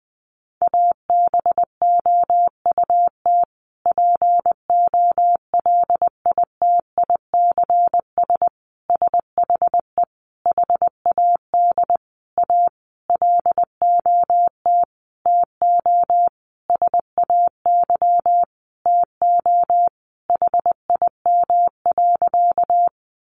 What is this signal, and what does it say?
Morse code, heard as a keyed tone, about 20 wpm, 700 Hz. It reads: ABOUT POLITICS SHE HAD A LOT TO SAY TO HIM.
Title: Sentences from Top 1000 Words 20wpm